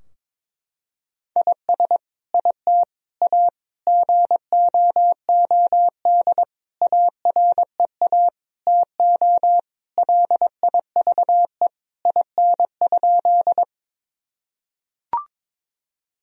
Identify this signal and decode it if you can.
Morse code, heard as a keyed tone, about 22 wpm, 700 Hz. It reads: IS IT A GOOD AREA TO LIVE IN?
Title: Sentences from Top 1000 Words - Review Part 2 - 22wpm